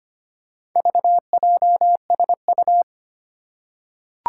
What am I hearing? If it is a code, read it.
Morse code, heard as a keyed tone, about 25 wpm, 700 Hz. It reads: VJSU
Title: Four Challenging Letter-Number Combinations 25wpm